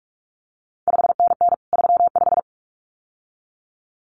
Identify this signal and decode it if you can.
Morse code, heard as a keyed tone, about 46 wpm, 700 Hz. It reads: 5NN 35
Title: CQ WW DX Contest Exchange - 45wpm